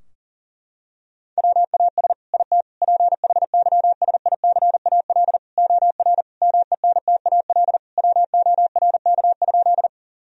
Morse code, heard as a keyed tone, about 40 wpm, 700 Hz. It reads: WAS IT PHYSICAL OR MENTAL WORK?
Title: Sentences from Top 1000 Words - Review Part 4 - 40wpm